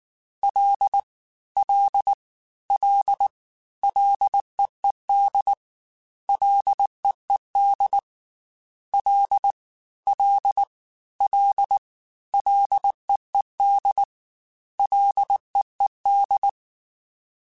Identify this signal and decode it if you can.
Morse code, heard as a keyed tone, about 19 wpm, 800 Hz. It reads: L L L LEED LEED L L L LEED LEED
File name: Ls / Single L m3s / Single L Prefixe 3x2 Twice @18